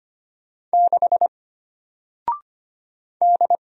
Morse code, heard as a keyed tone, about 25 wpm, 700 Hz. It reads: 6 D
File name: Single-Letter-Number-25wpm